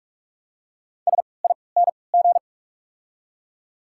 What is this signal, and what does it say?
Morse code, heard as a keyed tone, about 43 wpm, spaced as for 25 wpm, 690 Hz. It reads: SING
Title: 3x Character Spacing - Top 500 Words - 45wpm